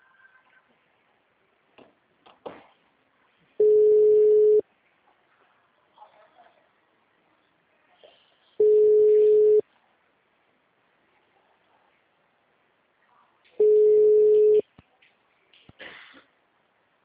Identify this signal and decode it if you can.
Ringback tone (Europe, 425 Hz), with speech.